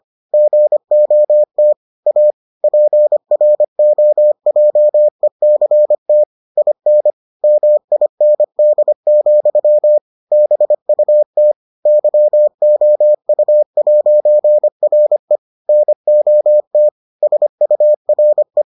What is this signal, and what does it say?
Morse code, heard as a keyed tone, about 25 wpm, 600 Hz. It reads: GOT A PROJECT IN MIND, BUT YOU'RE NOT SURE